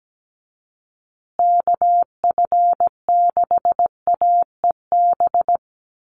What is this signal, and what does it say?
Morse code, heard as a keyed tone, about 17 wpm, 700 Hz. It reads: KF6AEB
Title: USA Call Signs Repeated 3x 17wpm